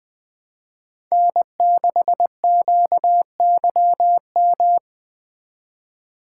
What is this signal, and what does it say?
Morse code, heard as a keyed tone, about 21 wpm, 700 Hz. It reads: N6QYM